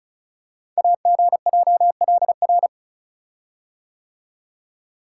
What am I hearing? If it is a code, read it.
Morse code, heard as a keyed tone, about 36 wpm, 700 Hz. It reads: AGJLR
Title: Code Groups - 5 Letters - 35wpm